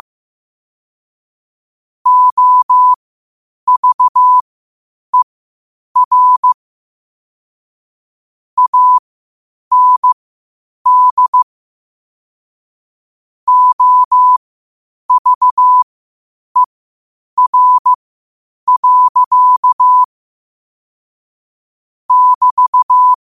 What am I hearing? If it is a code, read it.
Morse code, heard as a keyed tone, about 15 wpm, spaced as for 8 wpm, 1000 Hz. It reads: OVER AND OVER. <BT>
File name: qotdfiles / 8wpm